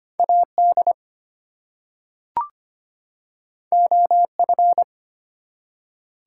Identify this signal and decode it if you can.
Morse code, heard as a keyed tone, about 25 wpm, 700 Hz. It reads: AD OF
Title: Two Letter Words - 25wpm